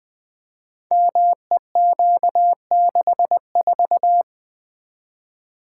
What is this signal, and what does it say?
Morse code, heard as a keyed tone, about 20 wpm, 700 Hz. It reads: MEQ64
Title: Code Groups - 5 Letter:Numbers - 20wpm